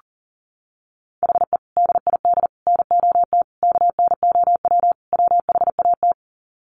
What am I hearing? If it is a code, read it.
Morse code, heard as a keyed tone, about 40 wpm, 700 Hz. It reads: HE DID NOT KNOW WHAT
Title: Sets of 5 Words Encore 40wpm